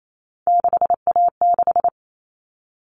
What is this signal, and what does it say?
Morse code, heard as a keyed tone, about 28 wpm, 700 Hz. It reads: 6A6
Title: All Cut Numbers - 3-Digit 28wpm